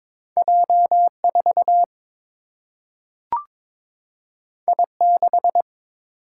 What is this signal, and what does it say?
Morse code, heard as a keyed tone, about 22 wpm, 700 Hz. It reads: J 4 I 6
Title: Single-Letter Single-Number 22wpm